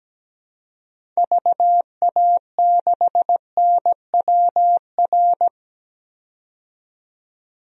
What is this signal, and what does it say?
Morse code, heard as a keyed tone, about 17 wpm, 700 Hz. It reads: VA6NWR
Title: DXCC-100 Call Signs - 6 Characters - 17wpm